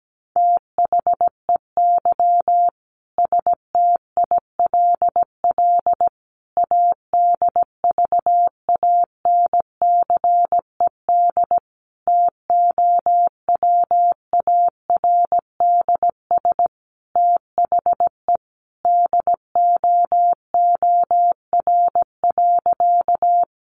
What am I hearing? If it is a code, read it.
Morse code, heard as a keyed tone, about 17 wpm, 700 Hz. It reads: THEY STILL ADVANCED TOWARDS THE DOOR.
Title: Sentences from Top 1200 Words 17wpm